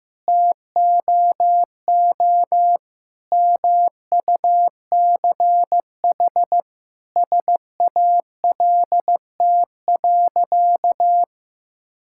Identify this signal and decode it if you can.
Morse code, heard as a keyed tone, about 15 wpm, 700 Hz. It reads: TOO MUCH SALT.
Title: Sentences from Top 1000 Words - Review Part 5 - 15wpm